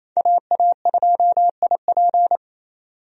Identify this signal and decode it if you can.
Morse code, heard as a keyed tone, about 28 wpm, 700 Hz. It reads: AA2IP
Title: USA Call Signs - 5 Characters - 28wpm